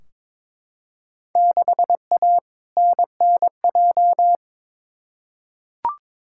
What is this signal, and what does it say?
Morse code, heard as a keyed tone, about 22 wpm, 700 Hz. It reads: 6A NNJ
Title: Field Day Exchanges Encore - 22wpm